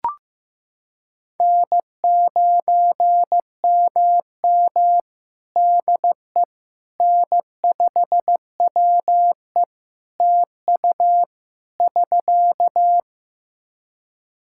Morse code, heard as a keyed tone, about 15 wpm, 700 Hz. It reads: N9MM DE N5WE TU <SK>
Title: Practice QSO - Exchange 4 - Ending - 15wpm